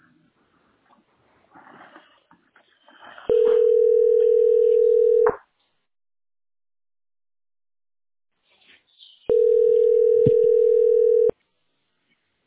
Ringback tone (North America), 440 + 480 Hz.